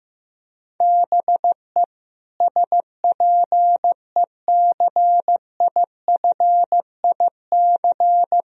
Morse code, heard as a keyed tone, about 15 wpm, 700 Hz. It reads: BE SPECIFIC
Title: Sentences from Top 1000 Words - Review Part 3 - 15wpm